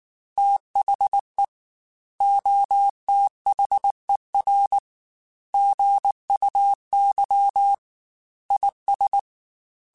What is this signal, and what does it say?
Morse code, heard as a keyed tone, about 19 wpm, 800 Hz. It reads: THE OTHER GUY IS